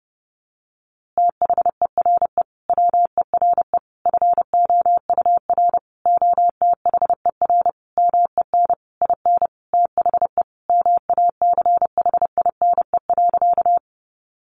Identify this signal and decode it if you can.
Morse code, heard as a keyed tone, about 30 wpm, 700 Hz. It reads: THERE WERE FOUR OTHER MEN IN THE MACHINE.
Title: A Sound Of Thunder 30wpm